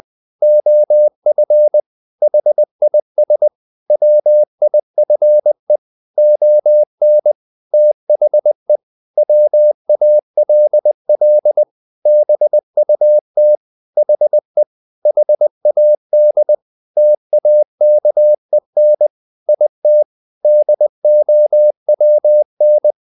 Morse code, heard as a keyed tone, about 20 wpm, 600 Hz. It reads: OF HIS WIFE ON THE WALL BUT HE HAD TAKEN IT DOWN
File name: the old man and the sea / the old man and the sea part 1 20wpm